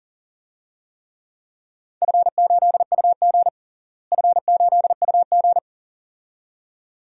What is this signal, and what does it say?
Morse code, heard as a keyed tone, about 39 wpm, 690 Hz. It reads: F8UG F8UG
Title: DXCC-100 Call Signs - 4 Characters - Repeated 2x - 40wpm